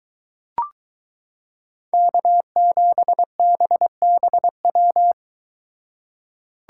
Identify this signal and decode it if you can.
Morse code, heard as a keyed tone, about 22 wpm, 700 Hz. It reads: K7BBW